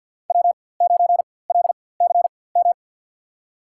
Morse code, heard as a keyed tone, about 50 wpm, spaced as for 23 wpm, 690 Hz. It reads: W9LCM
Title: USA Call Signs - 4x Character Spacing - 50wpm